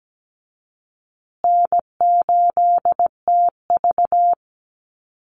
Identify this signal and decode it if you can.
Morse code, heard as a keyed tone, about 17 wpm, 700 Hz. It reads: N8TV
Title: USA Call Signs Repeated 3x 17wpm